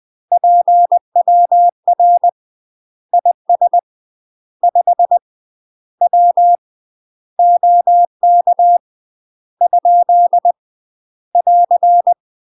Morse code, heard as a keyed tone, about 20 wpm, 700 Hz. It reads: PWR IS 5 W OK ? <AR>
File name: UK MASTER QSOs TRAINING / 20WPM / UK MASTER QSOs 0073